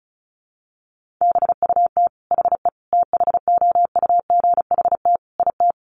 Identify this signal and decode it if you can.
Morse code, heard as a keyed tone, about 34 wpm, 700 Hz. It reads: BUT HE THOUGHT IT
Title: Sentences from Top 1400 Words 35wpm